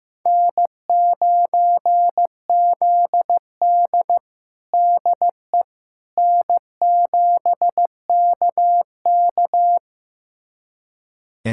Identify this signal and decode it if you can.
Morse code, heard as a keyed tone, about 15 wpm, 700 Hz. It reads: N9ZD DE N7KK
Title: USA Callsign DE Callsign - 4 Characters - 15wpm